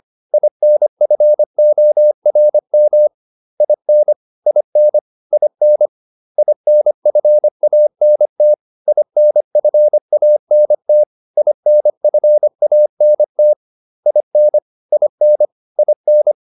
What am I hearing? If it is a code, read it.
Morse code, heard as a keyed tone, about 25 wpm, 600 Hz. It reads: INFORM IN IN IN INFANT INFANT INFANT IN IN IN